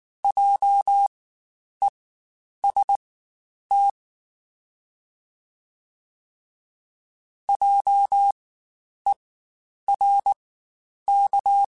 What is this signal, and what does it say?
Morse code, heard as a keyed tone, about 19 wpm, spaced as for 9 wpm, 800 Hz. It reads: JEST JERK
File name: Spaced Alphas / Spaced Alphas mp3s / Js sp @18